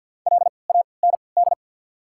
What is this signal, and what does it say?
Morse code, heard as a keyed tone, about 53 wpm, 690 Hz. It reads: L A N D
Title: 3x Character Spacing - Top 200 Words - 50wpm